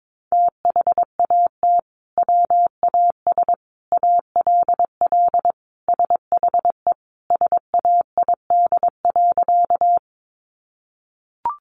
Morse code, heard as a keyed tone, about 22 wpm, 700 Hz. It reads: THAT WAS ALL SHE SAID.
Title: Sentences from Top 100 Words 22wpm